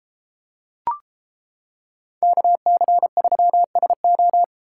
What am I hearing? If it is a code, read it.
Morse code, heard as a keyed tone, about 34 wpm, 700 Hz. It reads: KC3SO